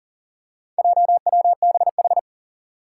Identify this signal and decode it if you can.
Morse code, heard as a keyed tone, about 38 wpm, 700 Hz. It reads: JWBH
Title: Four Challenging Letter-Number Combinations 40wpm